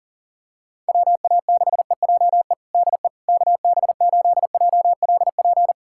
Morse code, heard as a keyed tone, about 40 wpm, 700 Hz. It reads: WA6EJE DE KB8JLP
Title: USA Callsign DE Callsign - 6 Characters - 40wpm